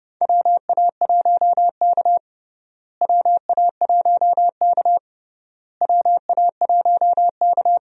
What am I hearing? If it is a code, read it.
Morse code, heard as a keyed tone, about 32 wpm, 700 Hz. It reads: WA1K WA1K WA1K
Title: USA Call Signs - 4 Characters - Repeated 3x - 30wpm